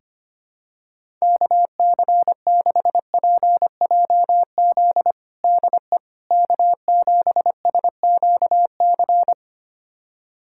Morse code, heard as a keyed tone, about 25 wpm, 700 Hz. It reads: KC6PJZ DE K7SQC
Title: USA Callsign DE Callsign 25wpm